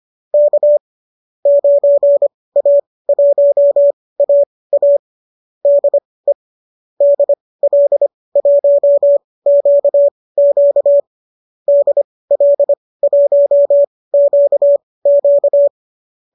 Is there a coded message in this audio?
Yes: Morse code from 0.3 to 15.7 s, heard as a keyed tone, about 25 wpm, spaced as for 18 wpm, 590 Hz. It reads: K 9A1AA DE DL1QQ DL1QQ